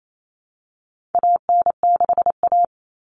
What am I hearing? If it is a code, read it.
Morse code, heard as a keyed tone, about 28 wpm, 700 Hz. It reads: AN6A
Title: Common Cut Numbers - 4-Digit 28wpm